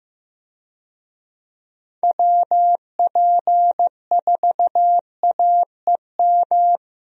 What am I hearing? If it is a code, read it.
Morse code, heard as a keyed tone, about 15 wpm, 700 Hz. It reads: WP4AEM
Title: USA Call Signs - 6 Characters - 15wpm